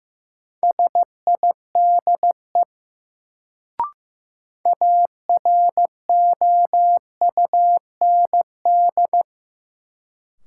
Morse code, heard as a keyed tone, about 15 wpm, 700 Hz. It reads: SIDE AROUND
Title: Sets of 2 Words - New Word Drill - 15wpm